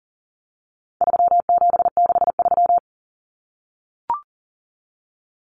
Morse code, heard as a keyed tone, about 40 wpm, 700 Hz. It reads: 3763
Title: Common Cut Numbers - 4-Digit 40wpm